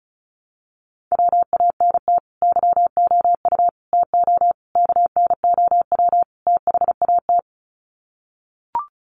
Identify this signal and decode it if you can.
Morse code, heard as a keyed tone, about 34 wpm, 700 Hz. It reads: WANT YOU TO KNOW THAT
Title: Sets of 5 Words Encore 35wpm